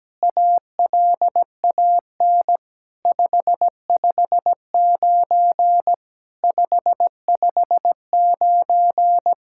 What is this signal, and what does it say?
Morse code, heard as a keyed tone, about 17 wpm, 700 Hz. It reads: ALAN 559 559
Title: SOTA QSO Practice 17wpm